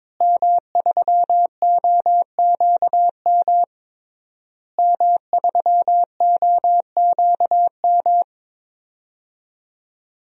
Morse code, heard as a keyed tone, about 22 wpm, 700 Hz. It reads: M3OQM M3OQM
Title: DXCC-100 Call Signs - 5 Characters - Repeated 3x - 22wpm